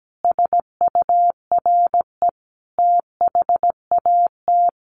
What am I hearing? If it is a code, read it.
Morse code, heard as a keyed tone, about 17 wpm, 700 Hz. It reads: SURE THAT
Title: Sets of 5 Words 17wpm